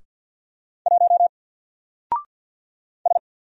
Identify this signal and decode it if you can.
Morse code, heard as a keyed tone, about 49 wpm, 700 Hz. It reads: 1 S
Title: Single Character (Letter:Number:Punctuation:Prosign) 50wpm